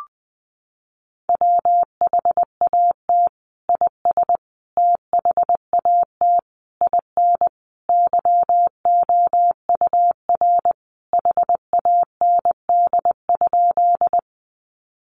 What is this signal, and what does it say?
Morse code, heard as a keyed tone, about 20 wpm, 700 Hz. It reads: WHAT IS THAT IN YOUR HAND?
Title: Sentences from Top 200 Words 20wpm